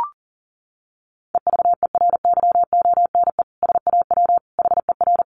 Morse code, heard as a keyed tone, about 40 wpm, 700 Hz. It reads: EVERYONE SAW HER
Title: Sentences from Top 800 Words 40wpm